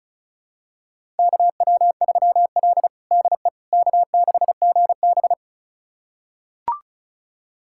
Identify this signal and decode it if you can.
Morse code, heard as a keyed tone, about 36 wpm, 700 Hz. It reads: KW3L DE K6GB
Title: USA Callsign DE Callsign - 4 Characters - 35wpm